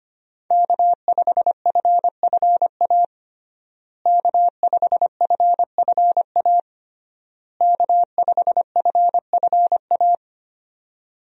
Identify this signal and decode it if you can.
Morse code, heard as a keyed tone, about 25 wpm, 700 Hz. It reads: K5FFA K5FFA K5FFA
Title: USA Call Signs - 5 Characters - Repeated 3x - 25wpm